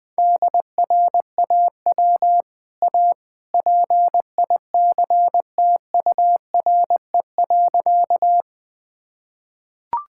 Morse code, heard as a keyed tone, about 20 wpm, 700 Hz. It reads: DRAW A PICTURE.
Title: Easy Sentences with Hints - 20wpm